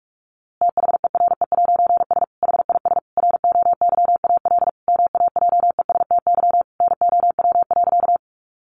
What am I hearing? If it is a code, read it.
Morse code, heard as a keyed tone, about 44 wpm, 700 Hz. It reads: THERE'S HIS ROYAL MAJESTY NOW.
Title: A Sound Of Thunder 45wpm